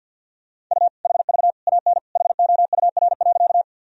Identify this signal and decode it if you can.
Morse code, heard as a keyed tone, about 48 wpm, 700 Hz. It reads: U HV AN HOUR.
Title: Sentences from Top 400 Words and Abbreviations 50wpm